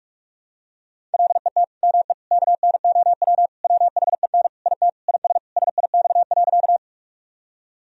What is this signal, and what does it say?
Morse code, heard as a keyed tone, about 41 wpm, 700 Hz. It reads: LET ME KNOW WHEN IT IS SIX.
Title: Sentences from Top 1000 Words - Review Part 2 - 45wpm